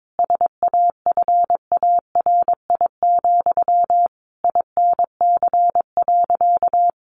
Morse code, heard as a keyed tone, about 22 wpm, 700 Hz. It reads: SAFARI, INC.
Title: A Sound Of Thunder 22wpm